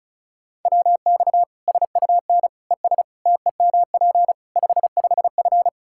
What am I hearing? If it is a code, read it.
Morse code, heard as a keyed tone, about 36 wpm, 700 Hz. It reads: WX SUN ES TEMP 55F
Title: Practice QSO - Exchange 2 - Info - 35wpm